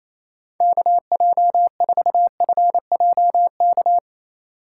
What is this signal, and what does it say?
Morse code, heard as a keyed tone, about 28 wpm, 700 Hz. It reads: KJ4FJK
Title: USA Call Signs - 6 Characters - 2x Character Spacing - 28wpm